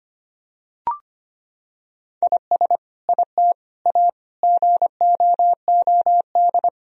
Morse code, heard as a keyed tone, about 25 wpm, 700 Hz. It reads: IS IT A GOOD
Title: Sentences from Top 1000 Words - Review Part 2 - 25wpm